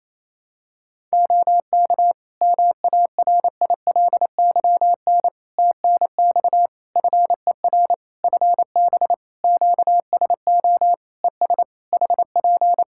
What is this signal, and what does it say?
Morse code, heard as a keyed tone, about 28 wpm, 700 Hz. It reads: OK MARILYN TNX FER FB QSO ES HP
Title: Practice QSO - Exchange 4 - Ending - 28wpm